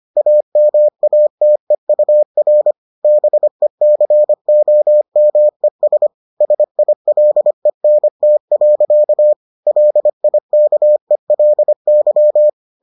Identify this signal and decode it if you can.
Morse code, heard as a keyed tone, about 25 wpm, 600 Hz. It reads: AMATEUR BECOMES SILENT. LIKELY